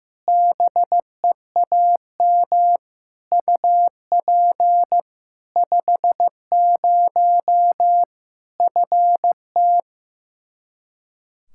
Morse code, heard as a keyed tone, about 15 wpm, 700 Hz. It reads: BEAM UP 50 FT
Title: Practice QSO - Exchange 2 - Info - 15wpm